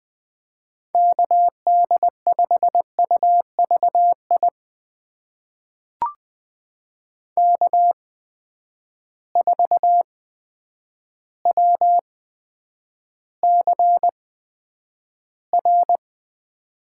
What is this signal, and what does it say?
Morse code, heard as a keyed tone, about 20 wpm, spaced as for 9 wpm, 700 Hz. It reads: KD5UVI K4WCR
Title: USA Call Signs - 8x Character Spacing - 20wpm